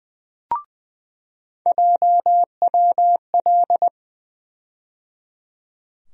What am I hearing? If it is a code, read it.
Morse code, heard as a keyed tone, about 21 wpm, 700 Hz. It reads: JWL
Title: USA Call Sign Postfixes - 3 Letters 20wpm